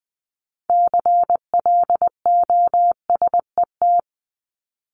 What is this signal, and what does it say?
Morse code, heard as a keyed tone, about 20 wpm, 700 Hz. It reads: CLOSET
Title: Top 1751-2000 Words 20wpm